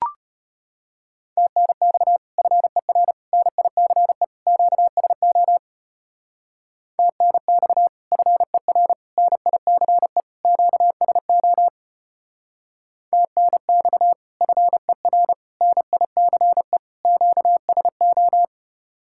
Morse code, heard as a keyed tone, about 32 wpm, 700 Hz. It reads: TNX FER NICE QSO TNX FER NICE QSO TNX FER NICE QSO